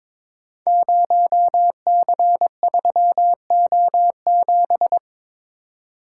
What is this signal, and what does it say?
Morse code, heard as a keyed tone, about 22 wpm, 700 Hz. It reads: 0C3O7
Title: Code Groups - 5 Letter:Numbers - 22wpm